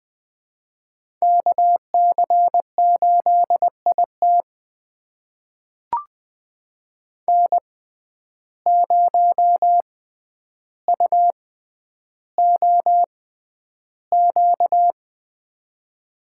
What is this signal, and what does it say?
Morse code, heard as a keyed tone, about 20 wpm, spaced as for 10 wpm, 700 Hz. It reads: KC8IT N0UOQ
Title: USA Call Signs - 5 Characters - 6x Character Spacing - 20wpm